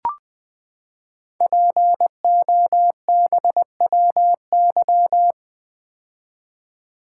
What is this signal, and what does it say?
Morse code, heard as a keyed tone, about 20 wpm, 700 Hz. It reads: POBWY
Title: Code Groups - 5 Letters - 20wpm